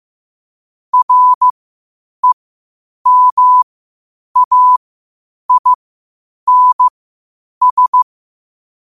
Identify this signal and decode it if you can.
Morse code, heard as a keyed tone, about 15 wpm, spaced as for 8 wpm, 1000 Hz. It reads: REMAINS
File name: qotdfiles / 8wpm